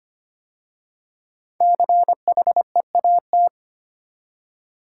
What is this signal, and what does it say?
Morse code, heard as a keyed tone, about 25 wpm, 700 Hz. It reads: CHEAT
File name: 5 Letter Words - First 3 Letters Given - 25wpm